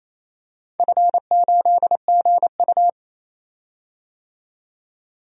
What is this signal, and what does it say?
Morse code, heard as a keyed tone, about 28 wpm, 700 Hz. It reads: F8GU
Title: DXCC-100 Call Signs - 4 Characters - 28wpm